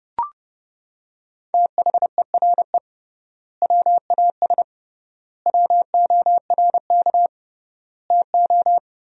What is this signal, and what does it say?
Morse code, heard as a keyed tone, about 30 wpm, 700 Hz. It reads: THERE WAS WORK TO
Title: 3x Word-Spacing - Sentences from Top 200 Words - 30wpm